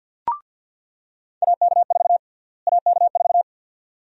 Morse code, heard as a keyed tone, about 47 wpm, 700 Hz. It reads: AK4 AK4
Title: USA Call Sign Prefixes - Repeated 2x - 50wpm